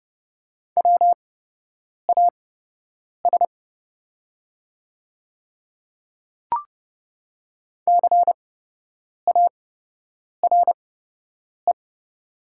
Morse code, heard as a keyed tone, about 30 wpm, spaced as for 8 wpm, 700 Hz. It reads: WAS CARE
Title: 8x Character Spacing - Top 300 Words - 30wpm